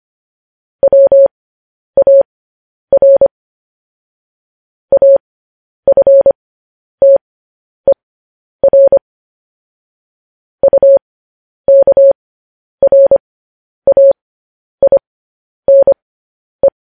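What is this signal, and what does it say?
Morse code, heard as a keyed tone, about 25 wpm, spaced as for 10 wpm, 550 Hz. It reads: WAR AFTER UKRAINE
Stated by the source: Morse code news headlines practice transmission.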